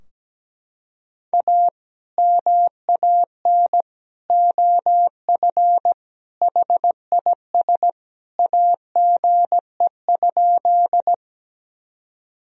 Morse code, heard as a keyed tone, about 17 wpm, 700 Hz. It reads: A MAN OF HIS AGE?
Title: Sentences from Top 1000 Words - Review Part 3 - 17wpm